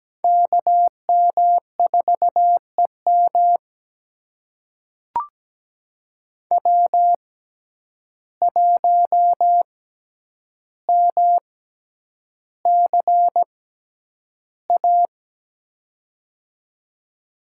Morse code, heard as a keyed tone, about 17 wpm, spaced as for 9 wpm, 700 Hz. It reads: KM4EM W1MCA
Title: USA Call Signs - 6x Character Spacing - 17wpm